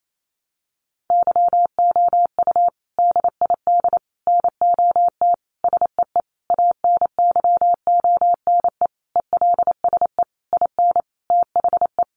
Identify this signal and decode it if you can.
Morse code, heard as a keyed tone, about 28 wpm, 700 Hz. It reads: YOU DID NOT SEE ANYONE ELSE IN THE
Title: Sentences from Top 900 Words 28wpm